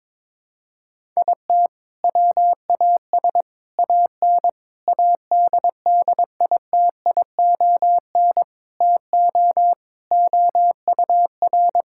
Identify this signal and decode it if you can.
Morse code, heard as a keyed tone, about 22 wpm, 700 Hz. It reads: IT WAS AN ADDITION TO OUR
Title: Sentences from Top 1000 Words - Review Part 6 - 22wpm